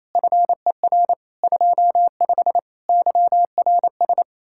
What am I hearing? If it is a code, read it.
Morse code, heard as a keyed tone, about 34 wpm, 700 Hz. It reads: FER 25 YRS
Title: Speed-Racing Common QSO Phrases Encore - 28wpm